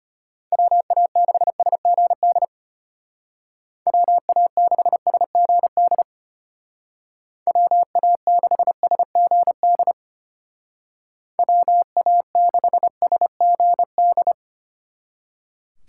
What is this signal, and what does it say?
Morse code, heard as a keyed tone, about 32 wpm, 700 Hz. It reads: WA6SGD WA6SGD WA6SGD WA6SGD